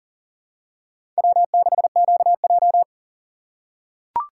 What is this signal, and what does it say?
Morse code, heard as a keyed tone, about 38 wpm, 700 Hz. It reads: W6QJ
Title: USA Call Signs - 4 Characters - 3x Character Spacing - 40wpm